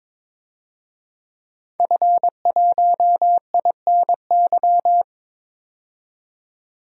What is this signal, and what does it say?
Morse code, heard as a keyed tone, about 21 wpm, 700 Hz. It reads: F1INY
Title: DXCC-100 Call Signs - 5 Characters - 22wpm